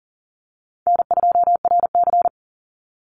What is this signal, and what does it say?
Morse code, heard as a keyed tone, about 40 wpm, 700 Hz. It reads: N2RC
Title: Easy Call Signs - 40wpm